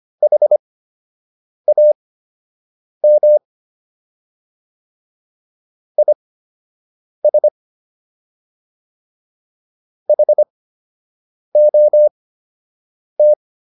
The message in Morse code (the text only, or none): HAM IS HOT